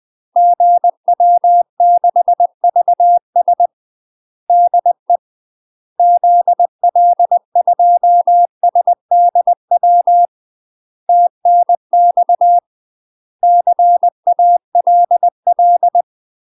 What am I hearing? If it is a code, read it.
Morse code, heard as a keyed tone, about 20 wpm, 700 Hz. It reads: GW6VS DE ZL2SDW TNX CALL
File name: UK MASTER QSOs TRAINING / 20WPM / UK MASTER QSOs 0002